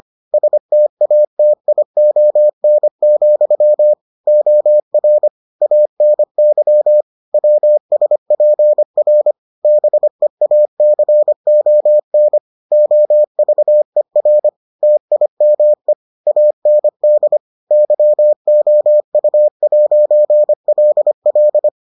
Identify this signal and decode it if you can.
Morse code, heard as a keyed tone, about 25 wpm, 600 Hz. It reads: STATION, OR ANY WSPR BEACON OVER TIME AND YOU'LL